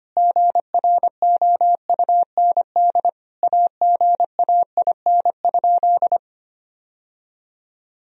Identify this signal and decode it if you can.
Morse code, heard as a keyed tone, about 25 wpm, 700 Hz. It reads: GROUND AGAIN?
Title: Sentences from Top 1000 Words - Review Part 2 - 25wpm